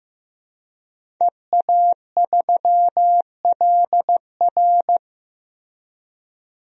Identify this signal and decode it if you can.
Morse code, heard as a keyed tone, about 15 wpm, 700 Hz. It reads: EA3LR
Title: DXCC-100 Call Signs - 5 Characters - 15wpm